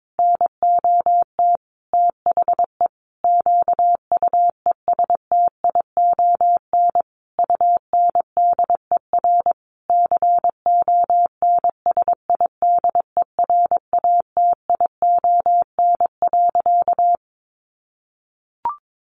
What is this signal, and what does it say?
Morse code, heard as a keyed tone, about 22 wpm, 700 Hz. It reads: NOT THE QUESTION UNDER CONSIDERATION.
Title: Sentences from Top 1500 Words 22wpm